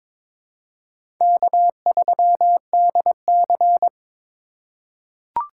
Morse code, heard as a keyed tone, about 22 wpm, 700 Hz. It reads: K3DC
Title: USA Call Signs - 4 Characters - 2x Character Spacing - 22wpm